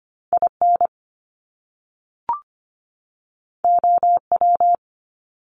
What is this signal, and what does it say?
Morse code, heard as a keyed tone, about 25 wpm, 700 Hz. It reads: IN OW
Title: Top Two Letter Combinations in Words 25wpm